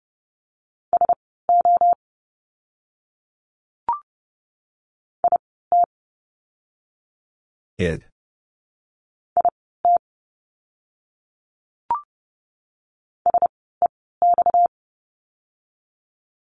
Morse code, heard as a keyed tone, about 30 wpm, spaced as for 17 wpm, 700 Hz. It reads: SO IT IT SEX
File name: course / 30:10wpm / 230 Lesson 37f - Letter x - Words 30wpm